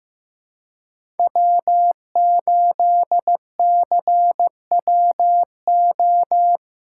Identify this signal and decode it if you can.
Morse code, heard as a keyed tone, about 16 wpm, 700 Hz. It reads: W8CWO
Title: USA Callsign DE Callsign - 5 Characters - 15wpm